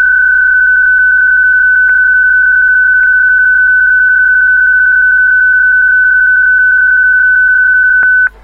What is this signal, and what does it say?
PSK31, about 31.25 baud.